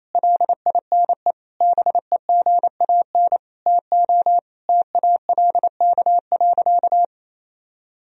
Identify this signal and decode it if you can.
Morse code, heard as a keyed tone, about 28 wpm, 700 Hz. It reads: LINE BEGAN TO TALK.
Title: Sentences from Top 1000 Words - Review Part 1 - 28wpm